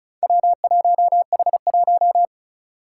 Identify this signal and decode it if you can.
Morse code, heard as a keyed tone, about 36 wpm, 700 Hz. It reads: W1H1
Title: Four Challenging Letter-Number Combinations 35wpm